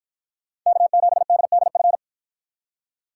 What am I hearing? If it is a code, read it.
Morse code, heard as a keyed tone, about 53 wpm, 700 Hz. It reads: K7DDF